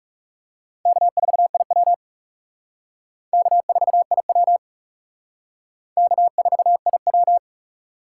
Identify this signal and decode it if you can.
Morse code, heard as a keyed tone, about 36 wpm, 700 Hz. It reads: K4IW K4IW K4IW
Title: Speed Racing - USA 1x2 Call Signs 30wpm